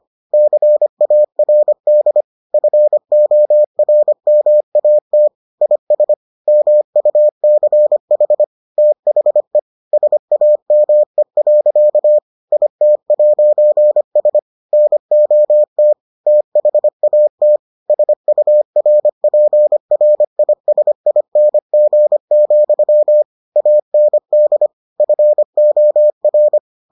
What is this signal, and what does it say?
Morse code, heard as a keyed tone, about 25 wpm, 600 Hz. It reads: CARD FORMAT IS MUCH THE SAME. IT'S NOT THAT SURPRISING, AND FOR